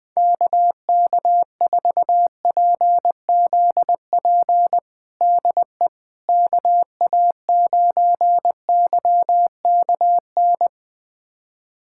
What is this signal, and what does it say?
Morse code, heard as a keyed tone, about 20 wpm, 700 Hz. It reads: KK4PZP DE KA9YKN
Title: USA Callsign DE Callsign - 6 Characters - 20wpm